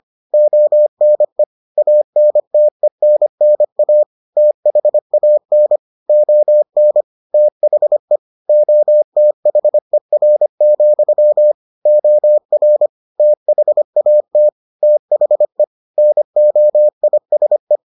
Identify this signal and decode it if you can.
Morse code, heard as a keyed tone, about 25 wpm, 600 Hz. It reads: ONE ANTENNA THAN ON THE OTHER, OR THAT THE NOISE